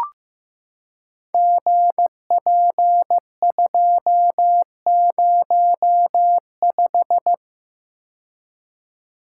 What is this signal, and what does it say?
Morse code, heard as a keyed tone, about 15 wpm, 700 Hz. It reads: GP205
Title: Code Groups - 5 Letter:Numbers - 15wpm